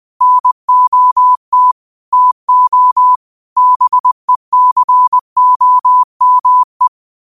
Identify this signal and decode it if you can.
Morse code, heard as a keyed tone, about 20 wpm, 1000 Hz. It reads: NOT TO BECOME
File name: qotdfiles / 20wpm